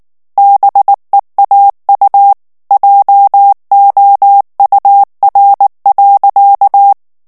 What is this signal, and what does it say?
Morse code, heard as a keyed tone, about 19 wpm, 800 Hz. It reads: BEAU JOUR.
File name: morse mp3 / FRANCAIS TEXTE CLAIR 1 VITESSE 18 MOTS MINUTES